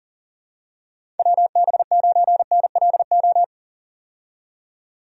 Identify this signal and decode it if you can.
Morse code, heard as a keyed tone, about 40 wpm, 700 Hz. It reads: WB9NLO
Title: USA Call Signs - 6 Characters - 40wpm